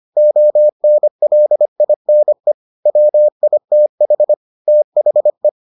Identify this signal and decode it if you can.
Morse code, heard as a keyed tone, about 25 wpm, 600 Hz. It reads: ONLINE WITH THE